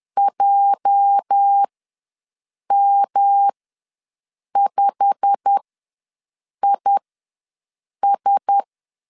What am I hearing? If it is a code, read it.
Morse code, heard as a keyed tone, about 10 wpm, spaced as for 6 wpm, 800 Hz. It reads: JM5IS
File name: LEZIONI / 8 YJ45 / 8 YJ45 ws 10wpm 3843